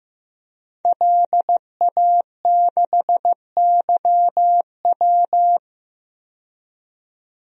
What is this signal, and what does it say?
Morse code, heard as a keyed tone, about 15 wpm, 700 Hz. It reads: LA6YW
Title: DXCC-100 Call Signs - 5 Characters - 15wpm